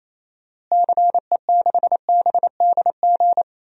Morse code, heard as a keyed tone, about 28 wpm, 700 Hz. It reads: CE6BDG
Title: DXCC-100 Call Signs - 6 Characters - Repeated 3x - 28wpm